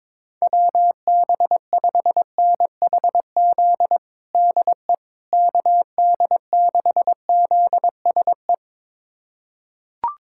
Morse code, heard as a keyed tone, about 21 wpm, 700 Hz. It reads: WB5NHZ DE KD6ZSE
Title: USA Callsign DE Callsign 22wpm